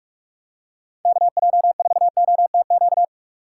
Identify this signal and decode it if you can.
Morse code, heard as a keyed tone, about 42 wpm, 700 Hz. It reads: KJ4OTQ
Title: USA Call Signs - 6 Characters - 4x Character Spacing - 45wpm